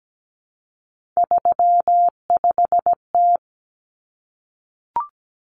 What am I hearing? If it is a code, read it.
Morse code, heard as a keyed tone, about 17 wpm, 700 Hz. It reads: 35T
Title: Common Cut Numbers - 3-Digit 17wpm